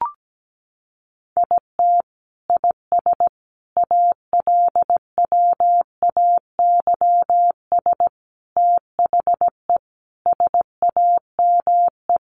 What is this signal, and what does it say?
Morse code, heard as a keyed tone, about 17 wpm, 700 Hz. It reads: IT IS ALWAYS THE SAME
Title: Sentences from Top 1400 Words 17wpm